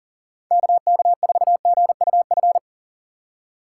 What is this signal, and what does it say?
Morse code, heard as a keyed tone, about 37 wpm, 700 Hz. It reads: KK4GUF
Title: USA Call Signs - 6 Characters - 4x Character Spacing - 40wpm